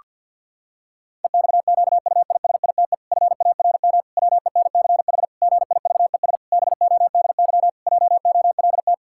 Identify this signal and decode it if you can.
Morse code, heard as a keyed tone, about 50 wpm, 700 Hz. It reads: EXQUISITE FARM WENCH GIVES BODY JOLT
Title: Pangram Challenge v2 - 50wpm